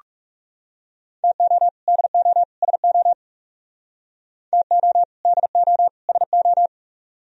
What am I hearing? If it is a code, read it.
Morse code, heard as a keyed tone, about 36 wpm, 700 Hz. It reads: TO DO SO TO DO SO
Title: Speed-Racing - Sets of 3 Words 30wpm